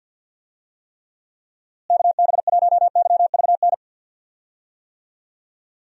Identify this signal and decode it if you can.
Morse code, heard as a keyed tone, about 52 wpm, 700 Hz. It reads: KB1YVN
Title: USA Call Signs - 6 Characters - Repeated 2x - 50wpm